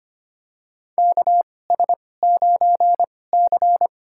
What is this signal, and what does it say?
Morse code, heard as a keyed tone, about 25 wpm, spaced as for 18 wpm, 700 Hz. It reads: KS9C
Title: USA Call Signs - 2x Character Spacing - 25wpm